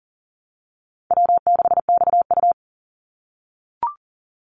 Morse code, heard as a keyed tone, about 40 wpm, 700 Hz. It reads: W6XU
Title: Easy Call Signs - 40wpm